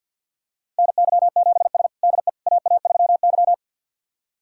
Morse code, heard as a keyed tone, about 46 wpm, 700 Hz. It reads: NY7S DE AA3Y
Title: USA Callsign DE Callsign - 4 Characters - 50wpm